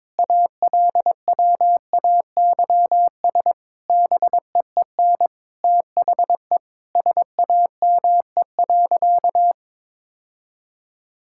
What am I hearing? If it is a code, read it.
Morse code, heard as a keyed tone, about 22 wpm, 700 Hz. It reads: ALWAYS BEEN THE SAME.
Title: Sentences from Top 1000 Words - Review Part 6 - 22wpm